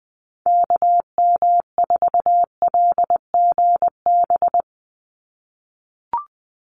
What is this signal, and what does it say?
Morse code, heard as a keyed tone, about 19 wpm, 700 Hz. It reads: KM4LGB
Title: USA Call Signs 20wpm